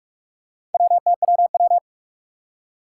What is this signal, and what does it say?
Morse code, heard as a keyed tone, about 44 wpm, 700 Hz. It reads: WTWW
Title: Four Challenging Letter-Number Combinations 45wpm